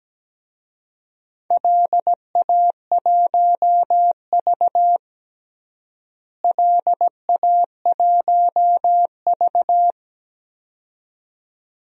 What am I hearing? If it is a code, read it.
Morse code, heard as a keyed tone, about 17 wpm, 700 Hz. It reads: LA1V LA1V
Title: DXCC-100 Call Signs - 4 Characters - Repeated 2x - 17wpm